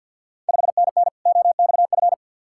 Morse code, heard as a keyed tone, about 50 wpm, 690 Hz. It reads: 5NN OXF
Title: Ontario QSO Party Exchange 50wpm